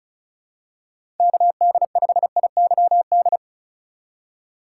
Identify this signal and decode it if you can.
Morse code, heard as a keyed tone, about 34 wpm, 700 Hz. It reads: KD5IYD
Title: USA Call Signs - 6 Characters - 4x Character Spacing - 35wpm